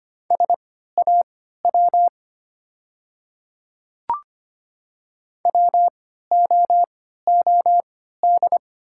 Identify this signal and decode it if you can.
Morse code, heard as a keyed tone, about 25 wpm, spaced as for 14 wpm, 700 Hz. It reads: SAW WOOD
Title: 3x Character Spacing - Top 300 Words - 25wpm